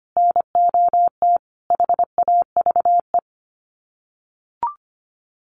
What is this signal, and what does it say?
Morse code, heard as a keyed tone, about 25 wpm, 700 Hz. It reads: NOT HAVE
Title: Sets of 5 Words Encore2 25wpm